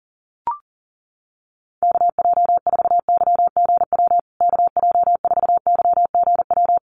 Morse code, heard as a keyed tone, about 40 wpm, 700 Hz. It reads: KJ4YGW KJ4YGW
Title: USA Call Signs Repeated 3x 40wpm